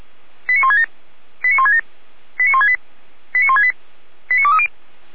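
Five-tone selective call (ZVEI), heard as quick runs of five tones, tones of about 70 ms. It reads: 89167 89167 89168 89168 89230
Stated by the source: ZVEI five-tone call received by SDR, from the Feuerwehr Orlamünde fire brigade archive.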